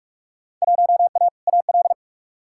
Morse code, heard as a keyed tone, about 42 wpm, 700 Hz. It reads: 1A AL
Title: Field Day Exchanges - 45wpm